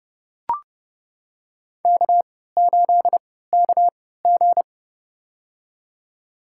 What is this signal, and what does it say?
Morse code, heard as a keyed tone, about 31 wpm, spaced as for 17 wpm, 700 Hz. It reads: K8KG
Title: USA Call Signs - 4 Characters - 3x Character Spacing - 30wpm